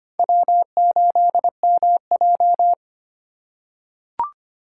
Morse code, heard as a keyed tone, about 25 wpm, 700 Hz. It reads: W8MJ
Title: USA Call Signs - 4 Characters - 3x Character Spacing - 25wpm